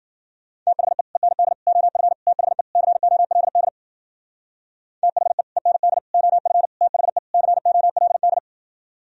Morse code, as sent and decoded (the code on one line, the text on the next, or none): - .... / . -. -.. / --- ..-. / - .... / -.-. --- .-.. -.. / - .... / . -. -.. / --- ..-. / - .... / -.-. --- .-.. -..
THE END OF THE COLD THE END OF THE COLD